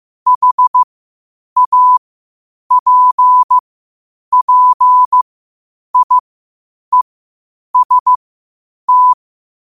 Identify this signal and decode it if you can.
Morse code, heard as a keyed tone, about 15 wpm, spaced as for 8 wpm, 1000 Hz. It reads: HAPPIEST